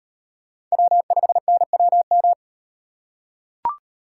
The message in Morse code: W5NWM

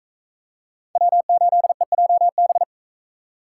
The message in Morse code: W8EJB

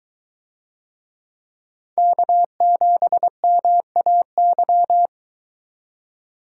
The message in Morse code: K7MAY

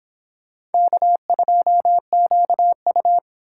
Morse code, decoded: K2QU

